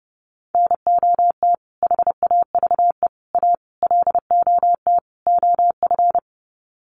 Morse code, heard as a keyed tone, about 28 wpm, 700 Hz. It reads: NOT HAVE A LOT OF
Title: Sets of 5 Words 30wpm